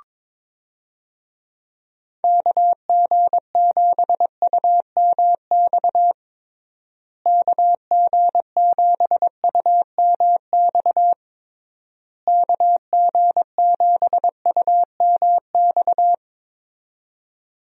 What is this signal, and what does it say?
Morse code, heard as a keyed tone, about 22 wpm, 700 Hz. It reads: KG7UMX KG7UMX KG7UMX
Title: USA Call Signs - 6 Characters - Repeated 3x - 22wpm